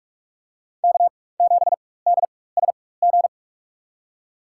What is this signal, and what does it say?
Morse code, heard as a keyed tone, about 47 wpm, spaced as for 21 wpm, 700 Hz. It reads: K7DSG
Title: USA Call Signs - 5 Characters - 4x Character Spacing - 45wpm